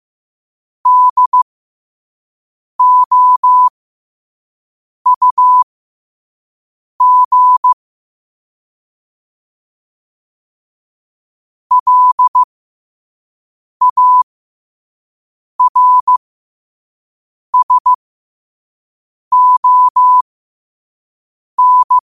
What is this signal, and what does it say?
Morse code, heard as a keyed tone, about 15 wpm, spaced as for 5 wpm, 1000 Hz. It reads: DOUG LARSON
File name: qotdfiles / 5wpm